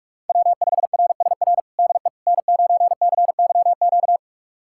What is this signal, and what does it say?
Morse code, heard as a keyed tone, about 46 wpm, 700 Hz. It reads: W5RSR DE N9CYQ
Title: USA Callsign DE Callsign - 5 Characters - 45wpm